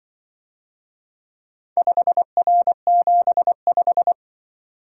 Morse code, heard as a keyed tone, about 23 wpm, 700 Hz. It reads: 5R75
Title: Speed-Racing - Four Letter-Number Combinations - 20wpm